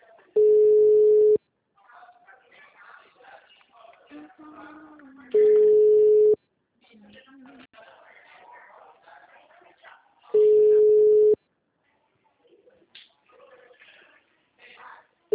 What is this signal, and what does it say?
Ringback tone (Europe, 425 Hz), with speech.